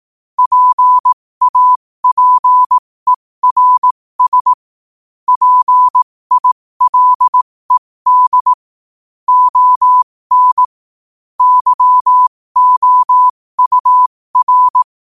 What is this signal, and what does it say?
Morse code, heard as a keyed tone, about 18 wpm, spaced as for 15 wpm, 1000 Hz. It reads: PAPERS PILED ON YOUR
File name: qotdfiles / 15wpm